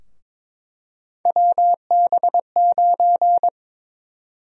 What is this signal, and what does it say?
Morse code, heard as a keyed tone, about 22 wpm, 700 Hz. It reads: WB9
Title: USA Call Sign Prefixes - Repeated 2x - 22wpm